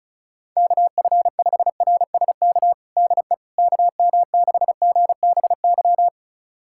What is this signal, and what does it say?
Morse code, heard as a keyed tone, about 36 wpm, 700 Hz. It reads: KF5RSK DE KM6GBY
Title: USA Callsign DE Callsign - 6 Characters - 35wpm